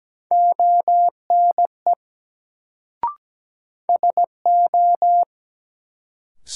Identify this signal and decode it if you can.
Morse code, heard as a keyed tone, about 17 wpm, 700 Hz. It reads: ONE SO
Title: Sentences from Top 100 Words - New Word Drill - 17wpm